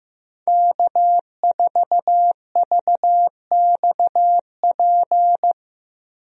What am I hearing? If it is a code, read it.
Morse code, heard as a keyed tone, about 16 wpm, 700 Hz. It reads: K4VXP
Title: USA Call Signs - 5 Characters - 15wpm